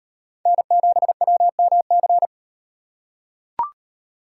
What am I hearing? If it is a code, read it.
Morse code, heard as a keyed tone, about 38 wpm, 700 Hz. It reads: N7WMC